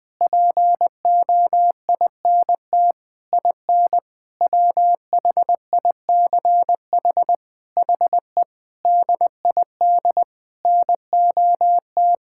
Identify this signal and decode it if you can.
Morse code, heard as a keyed tone, about 21 wpm, 700 Hz. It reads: POINT IN WHICH HE DID NOT
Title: Sentences from Top 1000 Words - Review Part 5 - 20wpm